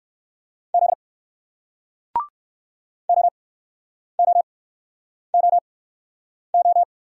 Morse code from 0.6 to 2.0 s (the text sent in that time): Z